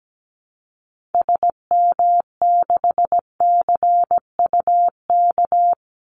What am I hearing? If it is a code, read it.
Morse code, heard as a keyed tone, about 17 wpm, 700 Hz. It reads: SM6CUK
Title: Call Signs Repeated 2x - 17wpm